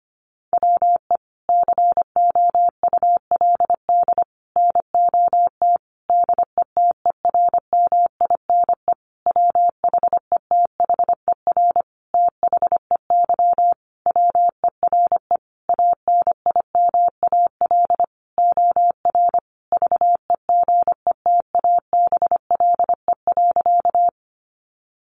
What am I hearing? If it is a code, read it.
Morse code, heard as a keyed tone, about 25 wpm, 700 Hz. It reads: WE COULD NOT DETERMINE WHETHER THEY WERE ANIMAL OR VEGETABLE.
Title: Sentences from Top 1300 Words 25wpm